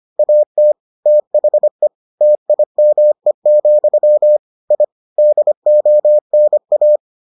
Morse code, heard as a keyed tone, about 25 wpm, 600 Hz. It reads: AT THE TIME, I DONA